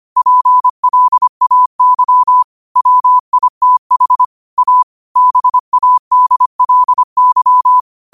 Morse code, heard as a keyed tone, about 25 wpm, 1000 Hz. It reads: PLAY WITH A BADLY